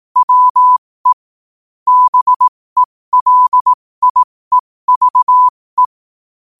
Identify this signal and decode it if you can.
Morse code, heard as a keyed tone, about 18 wpm, spaced as for 15 wpm, 1000 Hz. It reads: WE BELIEVE
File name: qotdfiles / 15wpm